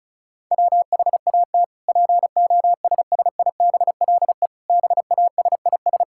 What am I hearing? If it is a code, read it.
Morse code, heard as a keyed tone, about 36 wpm, 700 Hz. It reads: WHAT POSSIBLE BASIS
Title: Sentences from Top 1000 Words - Review Part 5 - 35wpm